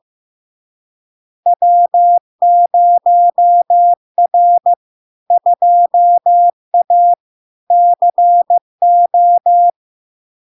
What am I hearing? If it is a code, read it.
Morse code, heard as a keyed tone, about 15 wpm, 700 Hz. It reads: W0R 2A CO
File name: LARK-FD-2022-Log-in-CW-15wpm-700-Hz